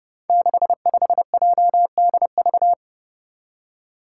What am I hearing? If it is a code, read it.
Morse code, heard as a keyed tone, about 30 wpm, 700 Hz. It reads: 65JDV
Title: Five Challenging Letter-Number Combinations 30wpm